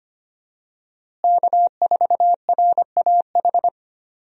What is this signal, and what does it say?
Morse code, heard as a keyed tone, about 25 wpm, 700 Hz. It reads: K4RAH
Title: USA Call Signs - 5 Characters - Repeated 3x - 25wpm